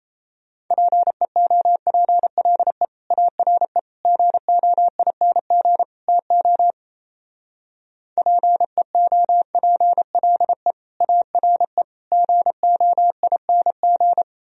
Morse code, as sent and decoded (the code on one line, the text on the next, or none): .--. --- .--. .-.. / .- .-. / --. --- .. -. --. / - --- / .--. --- .--. .-.. / .- .-. / --. --- .. -. --.
PEOPLE ARE GOING TO PEOPLE ARE GOING